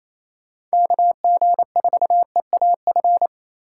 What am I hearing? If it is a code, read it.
Morse code, heard as a keyed tone, about 36 wpm, 700 Hz. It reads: KG4EAF